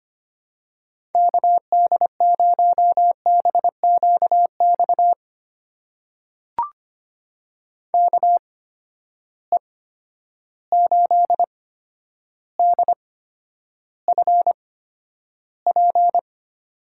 Morse code, heard as a keyed tone, about 25 wpm, spaced as for 11 wpm, 700 Hz. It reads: KD0BQX KE8DFP
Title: USA Call Signs - 6 Characters - 8x Character Spacing - 25wpm